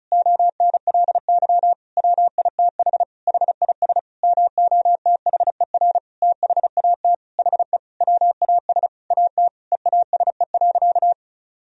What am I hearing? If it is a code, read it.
Morse code, heard as a keyed tone, about 36 wpm, 700 Hz. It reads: ONLY WITH HIS MOTHER THAT HE WAS AT EASE.
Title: Sentences from Top 1000 Words - Review Part 1 - 35wpm